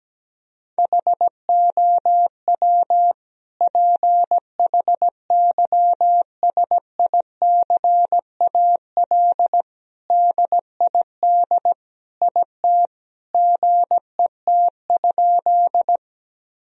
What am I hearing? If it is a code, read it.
Morse code, heard as a keyed tone, about 17 wpm, 700 Hz. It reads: HOW PHYSICAL DID IT GET?
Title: Sentences from Top 1000 Words - Review Part 4 - 17wpm